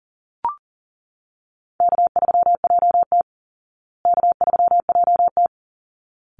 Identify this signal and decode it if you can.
Morse code, heard as a keyed tone, about 37 wpm, 700 Hz. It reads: K3JT K3JT
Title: Easy Call Signs Repeated 2x - 40wpm